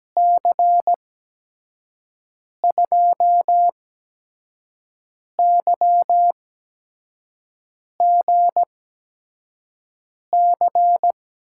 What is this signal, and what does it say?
Morse code, heard as a keyed tone, about 17 wpm, spaced as for 5 wpm, 700 Hz. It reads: C2YGC
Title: USA Call Signs - 6 Characters - 8x Character Spacing - 17wpm